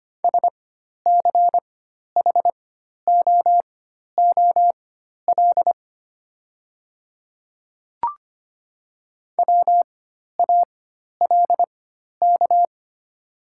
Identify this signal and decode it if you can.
Morse code, heard as a keyed tone, about 25 wpm, spaced as for 12 wpm, 700 Hz. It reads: SCHOOL WALK
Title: 4x Character Spacing - Top 500 Words - 25wpm